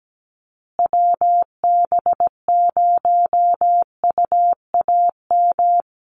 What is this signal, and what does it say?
Morse code, heard as a keyed tone, about 17 wpm, 700 Hz. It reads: WB0UAM
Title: USA Call Signs Repeated 2x 17wpm